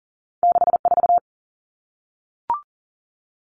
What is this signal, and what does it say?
Morse code, heard as a keyed tone, about 43 wpm, 700 Hz. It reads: 64
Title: Two Challenging Letter-Number Combinations 40wpm